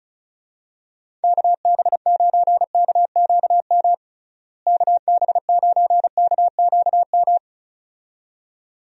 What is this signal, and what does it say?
Morse code, heard as a keyed tone, about 34 wpm, 700 Hz. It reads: KB9KQM KB9KQM
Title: USA Call Signs - 6 Characters - Repeated 2x - 35wpm